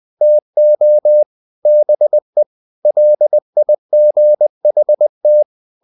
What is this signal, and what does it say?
Morse code, heard as a keyed tone, about 20 wpm, 600 Hz. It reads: TO BE LIGHT
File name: the old man and the sea / the old man and the sea part 1 20wpm